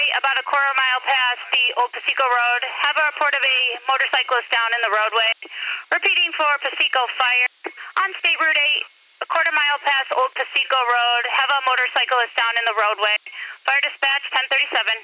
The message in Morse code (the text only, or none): none